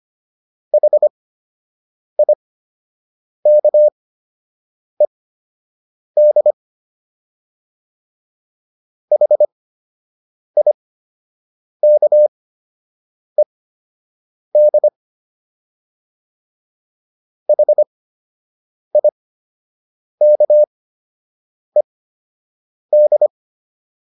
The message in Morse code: HIKED HIKED HIKED

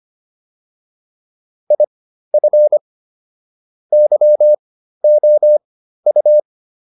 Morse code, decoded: IF YOU

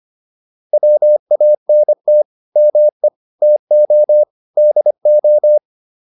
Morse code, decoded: WANT ME TO DO

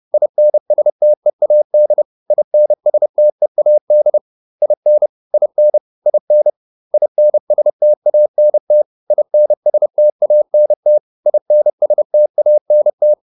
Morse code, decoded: INSTEAD INSTEAD IN IN IN INSTANT INSTANT INSTANT

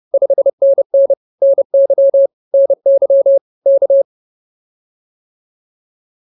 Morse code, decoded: 5NN NY NY K

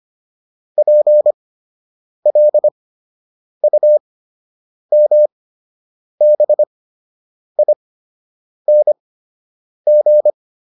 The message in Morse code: PLUMBING

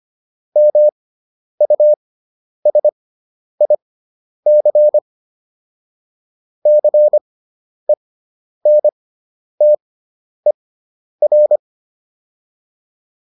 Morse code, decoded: MUSIC CENTER

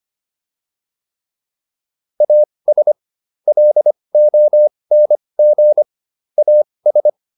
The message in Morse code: AS LONG AS

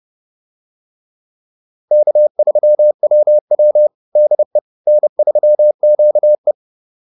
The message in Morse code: K3WW DE N3QE

